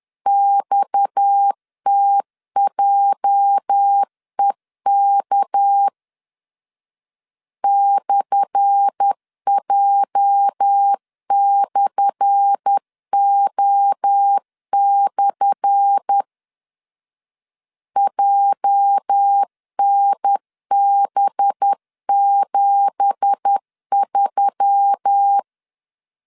Morse code, heard as a keyed tone, about 10 wpm, 800 Hz. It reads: XTJEK /J/O/ JNB73
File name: LEZIONI / 10 SIGNS / 10A SLASH 10wpm 2808